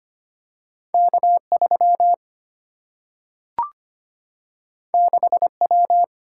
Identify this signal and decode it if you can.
Morse code, heard as a keyed tone, about 25 wpm, 700 Hz. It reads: K3 6W
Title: Single-Letter Single-Number 25wpm